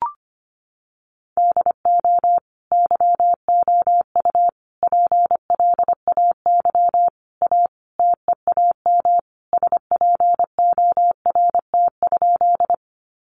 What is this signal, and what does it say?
Morse code, heard as a keyed tone, about 25 wpm, 700 Hz. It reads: DO YOU PLAY A TEAM SPORT?
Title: Sentences from Top 1500 Words 25wpm